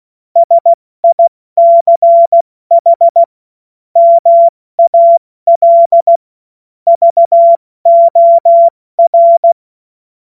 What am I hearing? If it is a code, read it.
Morse code, heard as a keyed tone, about 16 wpm, 680 Hz. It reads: SICH MAL VOR
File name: Morsekurs / Witze / Witz 08. August 2023 70 F80